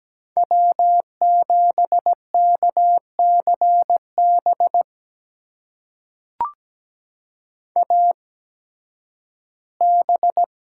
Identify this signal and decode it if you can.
Morse code, heard as a keyed tone, about 17 wpm, spaced as for 11 wpm, 700 Hz. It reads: W7KCB AB
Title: USA Call Signs - 8x Character Spacing - 17wpm